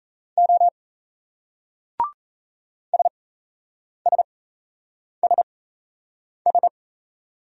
Morse code, heard as a keyed tone, about 38 wpm, spaced as for 8 wpm, 700 Hz. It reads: O SSSS